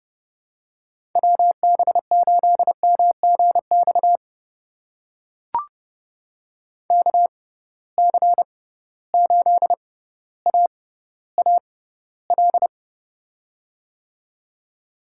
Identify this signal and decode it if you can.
Morse code, heard as a keyed tone, about 32 wpm, spaced as for 16 wpm, 700 Hz. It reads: WB8MGX KC8AAL
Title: USA Call Signs - 6 Characters - 6x Character Spacing - 30wpm